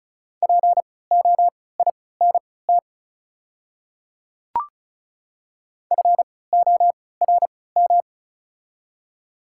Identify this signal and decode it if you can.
Morse code, heard as a keyed tone, about 36 wpm, spaced as for 20 wpm, 700 Hz. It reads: POINT FORM
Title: 3x Character Spacing - Top 200 Words - 35wpm